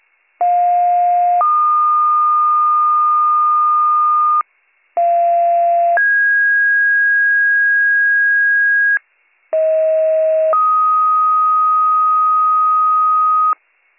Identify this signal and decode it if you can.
Two-tone page, heard as a one-second tone, then a three-second tone, about 710 then 1220 Hz, about 690 then 1650 Hz, about 640 then 1250 Hz. It is heard from 0.4 to 13.5 s.